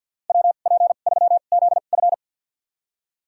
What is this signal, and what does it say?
Morse code, heard as a keyed tone, about 53 wpm, spaced as for 37 wpm, 700 Hz. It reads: WP3ZF
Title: USA Call Signs - 5 Characters - 2x Character Spacing - 50wpm